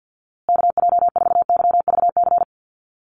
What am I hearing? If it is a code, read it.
Morse code, heard as a keyed tone, about 50 wpm, 700 Hz. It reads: KJ4YVC
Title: USA Call Signs Repeated 2x 50wpm